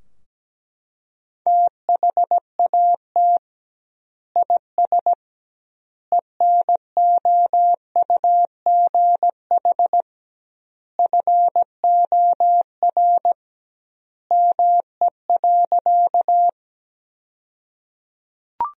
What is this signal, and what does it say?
Morse code, heard as a keyed tone, about 17 wpm, 700 Hz. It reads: THAT IS ENOUGH FOR ME.
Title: 2x Word-Spacing - Sentences from Top 400 Words - 17wpm